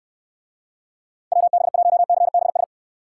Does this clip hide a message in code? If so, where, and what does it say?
Yes: Morse code from 1.3 to 2.6 s, heard as a keyed tone, about 67 wpm, 700 Hz. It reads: WB1CBS